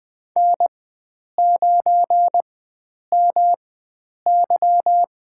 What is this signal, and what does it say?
Morse code, heard as a keyed tone, about 20 wpm, spaced as for 9 wpm, 700 Hz. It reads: N9MY
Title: USA Call Signs - 4x Character Spacing - 20wpm